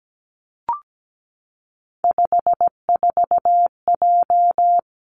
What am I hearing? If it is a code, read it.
Morse code, heard as a keyed tone, about 17 wpm, 700 Hz. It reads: 54J
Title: Three Challenging Letter-Number Combinations 17wpm